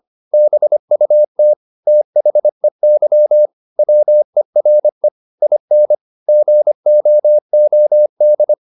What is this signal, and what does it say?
Morse code, heard as a keyed tone, about 25 wpm, 600 Hz. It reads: BUT THEY WERE IN GOOD
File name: the old man and the sea part 1 25wpm